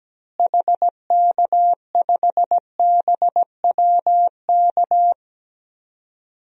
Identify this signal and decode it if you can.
Morse code, heard as a keyed tone, about 17 wpm, 700 Hz. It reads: HK5BWK
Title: DXCC-100 Call Signs - 6 Characters - Repeated 3x - 17wpm